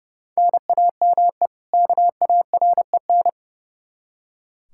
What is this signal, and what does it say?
Morse code, heard as a keyed tone, about 30 wpm, 700 Hz. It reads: NAME KAREN
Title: Practice QSO - Exchange 1 - Beginning - 30wpm